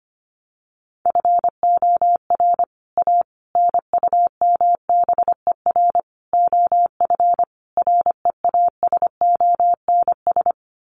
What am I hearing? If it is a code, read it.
Morse code, heard as a keyed tone, about 25 wpm, 700 Hz. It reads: FOR A NUMBER OF REASONS
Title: Sets of 5 Words Encore2 25wpm